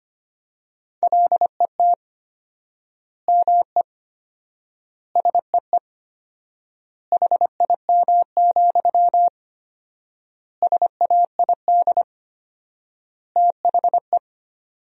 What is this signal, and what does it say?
Morse code, heard as a keyed tone, about 25 wpm, 700 Hz. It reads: LET ME SEE HIM, SAID THE
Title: 4x Word-Spacing - Sentences from Top 400 Words - 25wpm